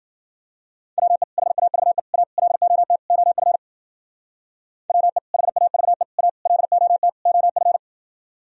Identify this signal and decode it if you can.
Morse code, heard as a keyed tone, about 49 wpm, 700 Hz. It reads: WE HAVE A LOT OF WE HAVE A LOT OF
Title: Speed-Racing - Sets of 5 Words 40wpm